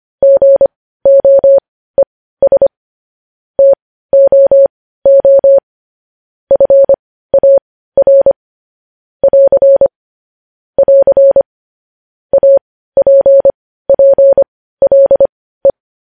Morse code, heard as a keyed tone, about 25 wpm, spaced as for 15 wpm, 550 Hz. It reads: GOES TOO FAR <AR> <AR> APPLE
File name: podcast / News headlines-1x-25x15